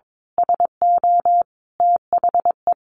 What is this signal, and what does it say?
Morse code, heard as a keyed tone, about 22 wpm, 700 Hz. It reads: SO THE
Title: A Sound Of Thunder 22wpm